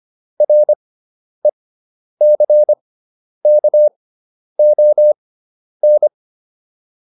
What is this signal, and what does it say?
Morse code, heard as a keyed tone, about 25 wpm, spaced as for 10 wpm, 600 Hz. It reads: RECKON
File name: RE6 10